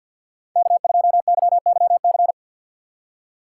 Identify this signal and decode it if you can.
Morse code, heard as a keyed tone, about 58 wpm, 700 Hz. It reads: K2YYC
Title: Speed Racing - USA 1x3 Call Signs 50wpm